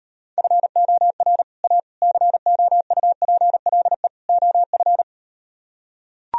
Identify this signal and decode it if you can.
Morse code, heard as a keyed tone, about 38 wpm, 700 Hz. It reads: FOR A COUPLE OF